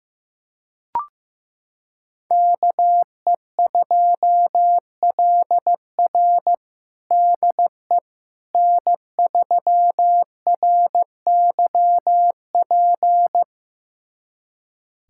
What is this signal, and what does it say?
Morse code, heard as a keyed tone, about 15 wpm, 700 Hz. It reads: KE2LR DE N3RYP
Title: USA Callsign DE Callsign - 5 Characters - 15wpm